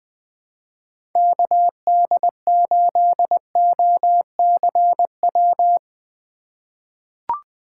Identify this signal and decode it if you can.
Morse code, heard as a keyed tone, about 20 wpm, 700 Hz. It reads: KD8OCW
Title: 2023 POTA Hunter Call Signs - 20wpm